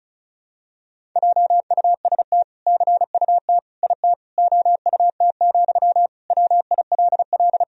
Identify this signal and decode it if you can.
Morse code, heard as a keyed tone, about 36 wpm, 700 Hz. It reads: JUST CUT IT OUT, WILL
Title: Sentences from Top 1000 Words - Review Part 1 - 35wpm